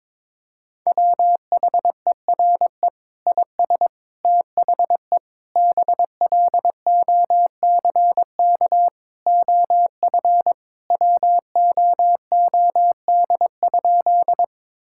Morse code, heard as a keyed tone, about 22 wpm, 700 Hz. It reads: WHERE IS THE BLOCK OF WOOD?
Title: Sentences from Top 1600 Words 22wpm